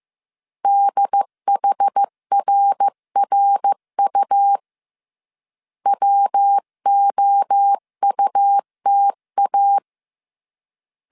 Morse code, heard as a keyed tone, about 15 wpm, 800 Hz. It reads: DHRRU WOUTA
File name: CW / DET / 1 3 1500C S C 14wpm 2458